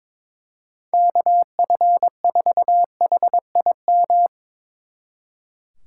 Morse code, heard as a keyed tone, about 24 wpm, 700 Hz. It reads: KF4HIM
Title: Speed Racing - USA Call Signs 22wpm